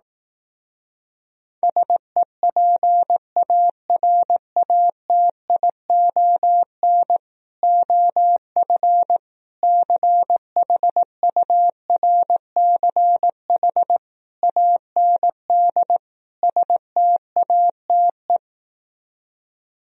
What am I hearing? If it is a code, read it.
Morse code, heard as a keyed tone, about 18 wpm, 700 Hz. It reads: SEPARATION OF CHURCH AND STATE
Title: Speed-Racing - Sets of 5 Words 15wpm